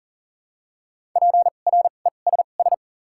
Morse code, heard as a keyed tone, about 41 wpm, spaced as for 29 wpm, 700 Hz. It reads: PRESS
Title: 2x Character Spacing - Top 300 Words - 40wpm